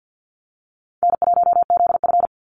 Morse code, heard as a keyed tone, about 50 wpm, 700 Hz. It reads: N1ZF
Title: NA SOTA Chaser Callsigns 50wpm